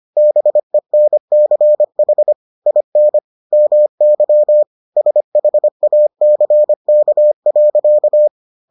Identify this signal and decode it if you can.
Morse code, heard as a keyed tone, about 25 wpm, 600 Hz. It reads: BENCH IN MY SHACK.